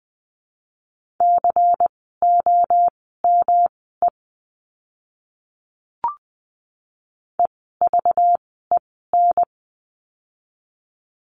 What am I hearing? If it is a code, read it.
Morse code, heard as a keyed tone, about 20 wpm, spaced as for 14 wpm, 700 Hz. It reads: COME EVEN
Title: 206 Lesson 33f - Letter j - Words 20wpm